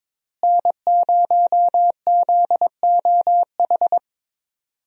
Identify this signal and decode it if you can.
Morse code, heard as a keyed tone, about 22 wpm, 700 Hz. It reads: N0ZOH
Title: USA Call Signs - 5 Characters - Repeated 2x - 22wpm